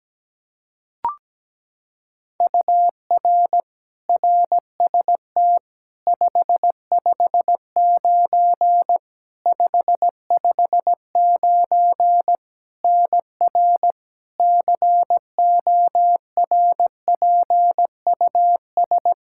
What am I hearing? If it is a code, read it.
Morse code, heard as a keyed tone, about 17 wpm, 700 Hz. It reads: UR RST 559 559 NR CORPUS
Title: K1USN SST QSO Practice 17wpm